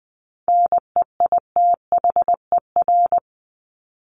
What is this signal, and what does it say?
Morse code, heard as a keyed tone, about 20 wpm, 700 Hz. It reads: NEITHER
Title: Top 1501-1750 Words 20wpm